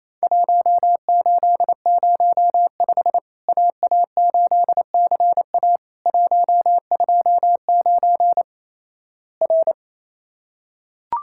Morse code, heard as a keyed tone, about 27 wpm, 700 Hz. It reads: 1805 AA8CA 129 R
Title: Worked All Europe DX Contest - QTC Exchange - 28wpm